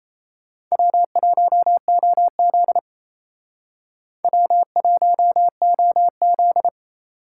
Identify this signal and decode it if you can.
Morse code, heard as a keyed tone, about 35 wpm, 700 Hz. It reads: W1OZ W1OZ